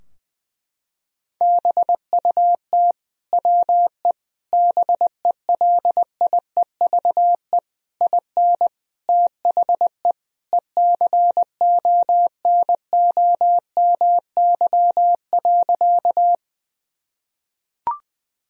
Morse code, heard as a keyed tone, about 19 wpm, 700 Hz. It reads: BUT WE BELIEVE IN THE ECONOMY.
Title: Sentences from Top 1000 Words - Review Part 3 - 20wpm